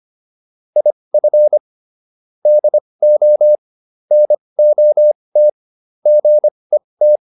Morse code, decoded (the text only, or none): IF DO NOT GET